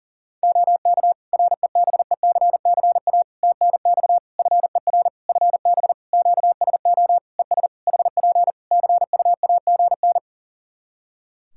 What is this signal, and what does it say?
Morse code, heard as a keyed tone, about 40 wpm, 700 Hz. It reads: OK REBECCA TNX FER FB QSO ES HP CUAGN
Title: Practice QSO - Exchange 4 - Ending - 40wpm